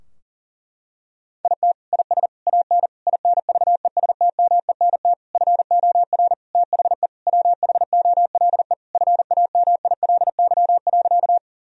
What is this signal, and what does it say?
Morse code, heard as a keyed tone, about 40 wpm, 700 Hz. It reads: IT IS AN INVESTMENT FOR THE WHOLE FAMILY.
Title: Sentences from Top 1000 Words - Review Part 4 - 40wpm